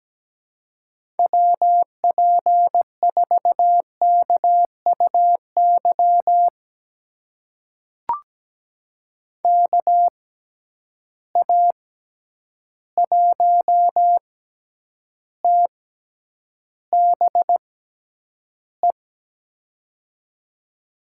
Morse code, heard as a keyed tone, about 17 wpm, spaced as for 9 wpm, 700 Hz. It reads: WP4KUY KA1TBE
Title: USA Call Signs - 6 Characters - 6x Character Spacing - 17wpm